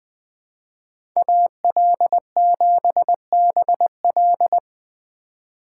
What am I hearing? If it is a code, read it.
Morse code, heard as a keyed tone, about 20 wpm, 700 Hz. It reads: AL7BL
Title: USA Call Signs - 5 Characters - 3x Character Spacing - 20wpm